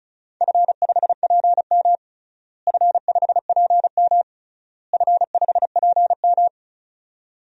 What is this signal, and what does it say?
Morse code, heard as a keyed tone, about 34 wpm, 700 Hz. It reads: F5PM F5PM F5PM
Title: DXCC-100 Call Signs - 4 Characters - Repeated 3x - 35wpm